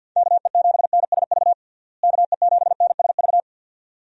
Morse code, heard as a keyed tone, about 50 wpm, 700 Hz. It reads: KE7NSV KE7NSV
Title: USA Call Signs - 6 Characters - Repeated 2x - 50wpm